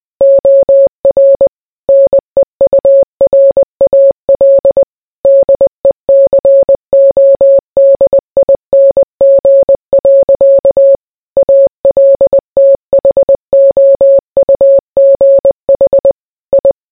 Morse code, heard as a keyed tone, about 20 wpm, 550 Hz. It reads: OR NEURAL DECODING. ALTHOUGH I